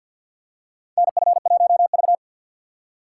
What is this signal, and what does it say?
Morse code, heard as a keyed tone, about 52 wpm, 700 Hz. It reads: NF1V
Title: USA Call Signs - 4 Characters - 6x Character Spacing - 50wpm